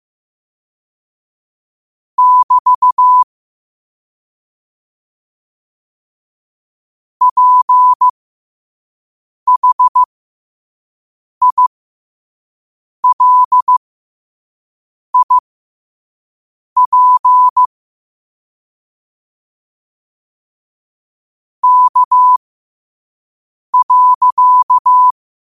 Morse code, heard as a keyed tone, about 15 wpm, spaced as for 5 wpm, 1000 Hz. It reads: <BT> PHILIP K.
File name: qotdfiles / 5wpm